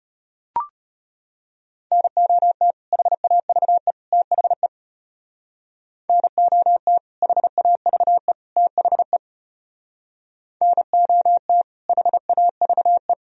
Morse code, heard as a keyed tone, about 31 wpm, 700 Hz. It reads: NOT HAVE THE NOT HAVE THE NOT HAVE
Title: Speed-Racing - Sets of 3 Words 25wpm